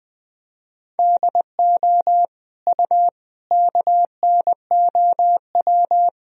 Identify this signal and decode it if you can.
Morse code, heard as a keyed tone, about 20 wpm, 700 Hz. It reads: DO U KNOW
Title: Sentences from Top 200 Words and Abbreviations 20wpm